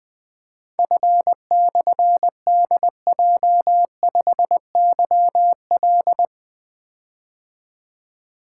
Morse code, heard as a keyed tone, about 20 wpm, 700 Hz. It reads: F/DJ5YL
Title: Difficult Call Signs 20wpm